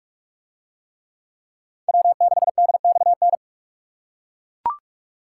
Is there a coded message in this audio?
Yes: Morse code from 1.9 to 3.4 s, heard as a keyed tone, about 40 wpm, 700 Hz. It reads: W6DXN